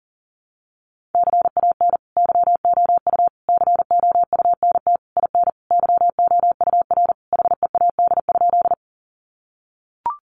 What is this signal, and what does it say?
Morse code, heard as a keyed tone, about 40 wpm, 700 Hz. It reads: CAN YOU COUNT IN YOUR HEAD?
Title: Sentences from Top 1500 Words 40wpm